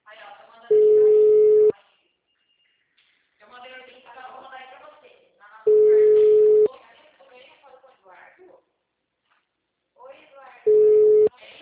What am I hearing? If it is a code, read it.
Ringback tone (Europe, 425 Hz), with speech.
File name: out- -2006- - - .21994